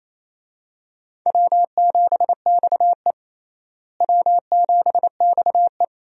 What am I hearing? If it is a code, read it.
Morse code, heard as a keyed tone, about 27 wpm, 700 Hz. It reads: W7XE W7XE
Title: USA Call Signs - 4 Characters - Repeated 3x - 28wpm